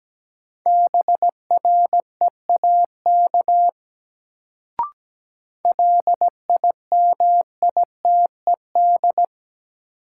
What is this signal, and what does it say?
Morse code, heard as a keyed tone, about 17 wpm, 700 Hz. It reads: BREAK LIMITED
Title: Sentences from Top 900 Words - New Word Drill - 17wpm